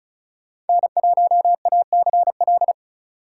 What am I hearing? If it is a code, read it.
Morse code, heard as a keyed tone, about 36 wpm, 700 Hz. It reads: N1ACL
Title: USA Call Signs - 2x Character Spacing - 35wpm